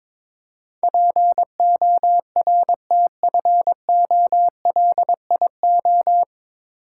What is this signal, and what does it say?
Morse code, heard as a keyed tone, about 22 wpm, 700 Hz. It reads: PORTFOLIO
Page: Morse Code Ninja practice 22wpm